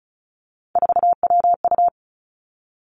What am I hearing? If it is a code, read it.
Morse code, heard as a keyed tone, about 36 wpm, 700 Hz. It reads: 4WU
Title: Three Challenging Letter-Number Combinations 35wpm